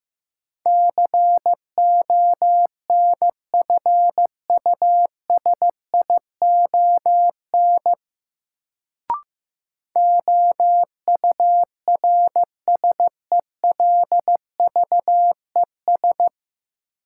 Morse code, heard as a keyed tone, about 16 wpm, 700 Hz. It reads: CONFUSION OURSELVES
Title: Sentences from Top 1600 Words - New Word Drill - 15wpm